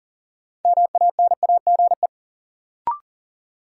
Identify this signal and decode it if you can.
Morse code, heard as a keyed tone, about 40 wpm, 700 Hz. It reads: MANAGE
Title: Sentences from Top 1400 Words - New Word Drill Alt - 40wpm